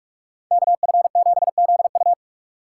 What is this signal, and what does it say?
Morse code, heard as a keyed tone, about 46 wpm, 700 Hz. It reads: KF7ZU
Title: USA Call Signs - 5 Characters - 4x Character Spacing - 45wpm